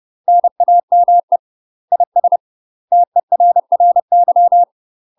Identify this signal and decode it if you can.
Morse code, heard as a keyed tone, about 30 wpm, 700 Hz. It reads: NAME IS TERRY